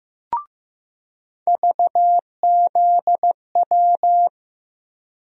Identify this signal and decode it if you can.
Morse code, heard as a keyed tone, about 15 wpm, 700 Hz. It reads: VZW
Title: USA Call Sign Postfixes - 3 Letters 15wpm